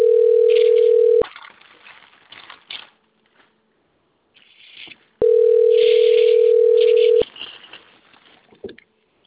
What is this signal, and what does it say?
Ringback tone (North America), 440 + 480 Hz.